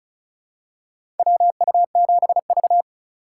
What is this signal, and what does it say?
Morse code, heard as a keyed tone, about 36 wpm, 700 Hz. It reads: WU7V
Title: USA Call Signs - 4 Characters - 3x Character Spacing - 35wpm